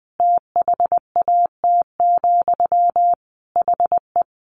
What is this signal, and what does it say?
Morse code, heard as a keyed tone, about 20 wpm, 700 Hz. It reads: THAT, HE
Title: Sentences from Top 1000 Words 20wpm